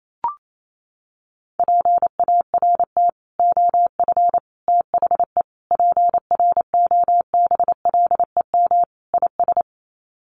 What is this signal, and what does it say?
Morse code, heard as a keyed tone, about 28 wpm, 700 Hz. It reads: PART OF THE PROBLEM IS
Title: Sets of 5 Words Encore2 28wpm